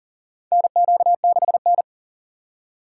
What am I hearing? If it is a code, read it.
Morse code, heard as a keyed tone, about 40 wpm, 700 Hz. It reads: NQ6N
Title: USA Call Signs - 4 Characters - 6x Character Spacing - 40wpm